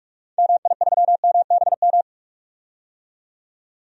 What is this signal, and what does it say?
Morse code, heard as a keyed tone, about 44 wpm, 700 Hz. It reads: MI3MBM
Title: DXCC-100 Call Signs - 6 Characters - 45wpm